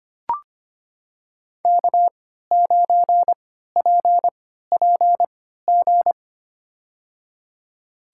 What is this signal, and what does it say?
Morse code, heard as a keyed tone, about 25 wpm, spaced as for 14 wpm, 700 Hz. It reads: K9PPG
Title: USA Call Signs - 5 Characters - 3x Character Spacing - 25wpm